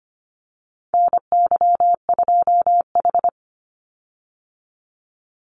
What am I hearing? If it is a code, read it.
Morse code, heard as a keyed tone, about 25 wpm, 700 Hz. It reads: NY2H
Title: Call Signs Repeated 2x - 25wpm